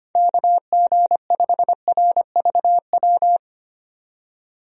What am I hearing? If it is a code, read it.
Morse code, heard as a keyed tone, about 25 wpm, 700 Hz. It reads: KG5RVW
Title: USA Call Signs - 6 Characters - 25wpm